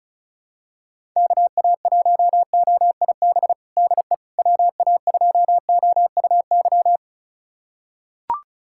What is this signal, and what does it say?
Morse code, heard as a keyed tone, about 35 wpm, 700 Hz. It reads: KA1OIB DE WA2OUY
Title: USA Callsign DE Callsign 35wpm